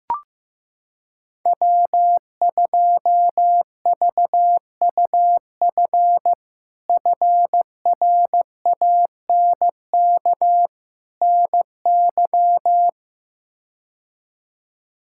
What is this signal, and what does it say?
Morse code, heard as a keyed tone, about 15 wpm, 700 Hz. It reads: W2VUF FRANK NY
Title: K1USN SST QSO Practice - 15wpm